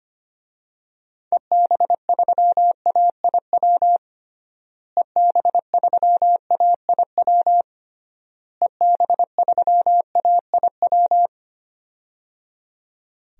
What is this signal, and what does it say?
Morse code, heard as a keyed tone, about 25 wpm, 700 Hz. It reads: EB3AIW EB3AIW EB3AIW
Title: DXCC-100 Call Signs - 6 Characters - Repeated 3x - 25wpm